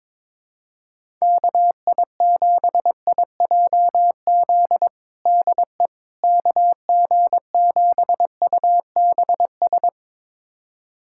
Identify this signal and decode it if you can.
Morse code, heard as a keyed tone, about 22 wpm, 700 Hz. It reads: KI7IJZ DE KG7UBS
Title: USA Callsign DE Callsign 22wpm